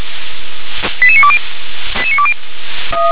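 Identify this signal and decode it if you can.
Five-tone selective call (ZVEI), heard as quick runs of five tones, tones of about 70 ms. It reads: 88920
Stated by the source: ZVEI five-tone call 88920 received on SDR.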